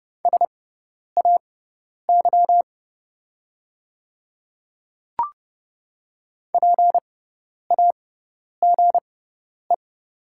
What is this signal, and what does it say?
Morse code, heard as a keyed tone, about 30 wpm, spaced as for 10 wpm, 700 Hz. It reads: SAY PAGE
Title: 6x Character Spacing - Top 500 Words - 30wpm